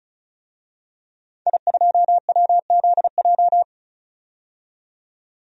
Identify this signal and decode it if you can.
Morse code, heard as a keyed tone, about 35 wpm, 700 Hz. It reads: I2WZJ
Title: DXCC-100 Call Signs - 5 Characters - 35wpm